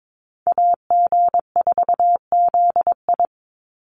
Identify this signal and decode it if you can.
Morse code, heard as a keyed tone, about 22 wpm, 700 Hz. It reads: AG4ZI
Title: USA Call Signs 22wpm